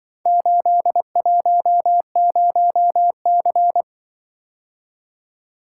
Morse code, heard as a keyed tone, about 24 wpm, 700 Hz. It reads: 810C
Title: Speed-Racing - Four Letter-Number Combinations - 20wpm